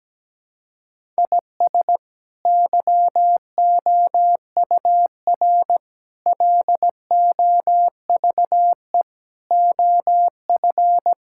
Morse code, heard as a keyed tone, about 17 wpm, 700 Hz. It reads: IS YOUR LOVE OF
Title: Sentences from Top 1000 Words - Review Part 5 - 17wpm